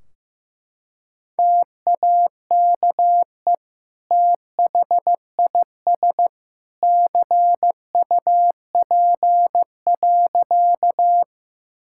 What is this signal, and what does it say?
Morse code, heard as a keyed tone, about 16 wpm, 700 Hz. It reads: TAKE THIS CUP.
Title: Sentences from Top 1600 Words 15wpm